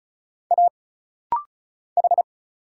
Morse code, heard as a keyed tone, about 34 wpm, 700 Hz. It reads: A H
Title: Single Letters - Rapid-Fire 35wpm